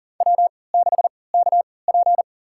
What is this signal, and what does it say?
Morse code, heard as a keyed tone, about 37 wpm, spaced as for 22 wpm, 700 Hz. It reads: W6KP
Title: USA Call Signs - 4 Characters - 3x Character Spacing - 40wpm